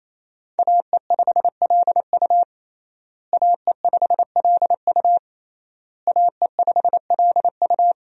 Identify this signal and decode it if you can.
Morse code, heard as a keyed tone, about 28 wpm, 700 Hz. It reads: AE5LU AE5LU AE5LU
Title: USA Call Signs - 5 Characters - Repeated 3x - 28wpm